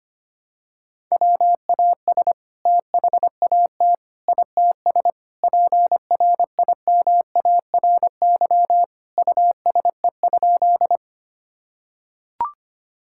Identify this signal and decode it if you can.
Morse code, heard as a keyed tone, about 25 wpm, 700 Hz. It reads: WAS THAT ITS PRIMARY USE?
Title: Sentences from Top 1000 Words - Review Part 5 - 25wpm